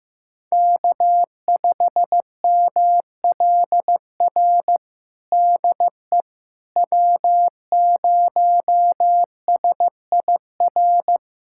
Morse code, heard as a keyed tone, about 16 wpm, 700 Hz. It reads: K5MLR DE W0SIR
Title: USA Callsign DE Callsign - 5 Characters - 15wpm